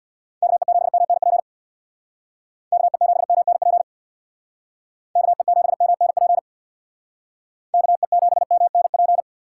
Morse code, heard as a keyed tone, about 60 wpm, 700 Hz. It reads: KE7MNP KE7MNP KE7MNP KE7MNP
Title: Speed Racing - USA Call Signs 50wpm